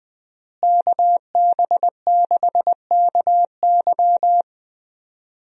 Morse code, heard as a keyed tone, about 20 wpm, 700 Hz. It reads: KB6KY
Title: USA Call Signs - 5 Characters - Repeated 3x - 20wpm